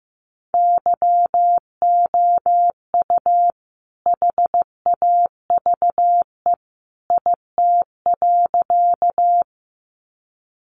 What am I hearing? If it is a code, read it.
Morse code, heard as a keyed tone, about 15 wpm, 700 Hz. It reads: YOU HAVE IT.
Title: Sentences from Top 100 Words 15wpm